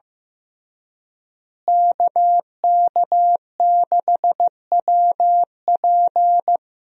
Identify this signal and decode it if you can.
Morse code, heard as a keyed tone, about 15 wpm, 700 Hz. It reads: KK6WP
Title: USA Call Signs - 5 Characters - Repeated 3x - 15wpm